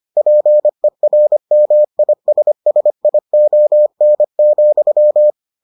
Morse code, heard as a keyed tone, about 25 wpm, 600 Hz. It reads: PERMISSION,